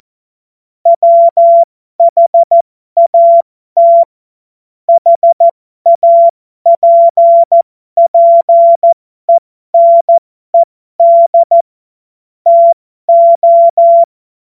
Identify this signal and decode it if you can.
Morse code, heard as a keyed tone, about 14 wpm, spaced as for 12 wpm, 680 Hz. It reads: WHAT HAPPENED TO